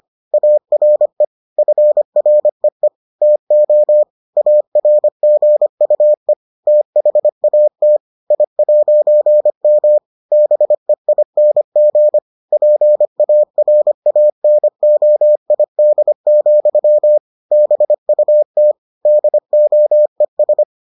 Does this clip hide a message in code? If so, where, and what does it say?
Yes: Morse code from 0.3 to 20.6 s, heard as a keyed tone, about 25 wpm, 600 Hz. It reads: ARE FREE TO ARGUE THAT I'M BEING PARANOID, BUT DOES